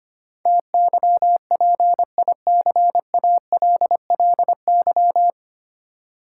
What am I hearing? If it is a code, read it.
Morse code, heard as a keyed tone, about 25 wpm, 700 Hz. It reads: TYPICALLY
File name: Nine Letter Words - 25wpm